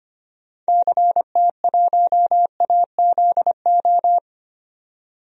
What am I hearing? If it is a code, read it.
Morse code, heard as a keyed tone, about 25 wpm, 700 Hz. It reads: CT1AZO
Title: DXCC-100 Call Signs - 6 Characters - 25wpm